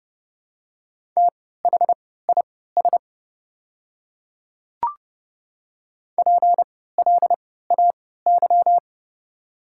Morse code, heard as a keyed tone, about 29 wpm, spaced as for 17 wpm, 690 Hz. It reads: THIS PLAY
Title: 3x Character Spacing - Top 200 Words - 30wpm